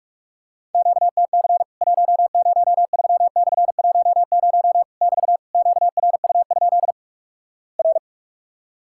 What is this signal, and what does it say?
Morse code, heard as a keyed tone, about 48 wpm, 700 Hz. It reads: QTC 103/10 <BT> QRU? R
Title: Worked All Europe DX Contest - QTC Exchange - 45wpm